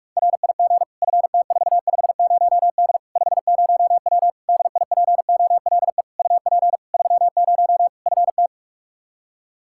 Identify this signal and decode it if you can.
Morse code, heard as a keyed tone, about 46 wpm, 700 Hz. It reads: RIG FT450D 50W DIPOLE UP 30 FT
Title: Practice QSO - Exchange 2 - Info - 45wpm